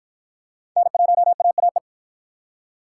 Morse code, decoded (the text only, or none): N1AAE